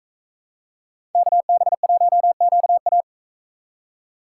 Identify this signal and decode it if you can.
Morse code, heard as a keyed tone, about 41 wpm, 700 Hz. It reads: KB1QA